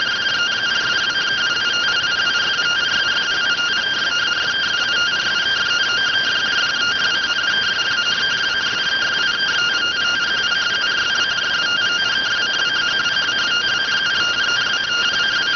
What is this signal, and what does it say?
Two-tone teleprinter signal, about 170 Hz shift.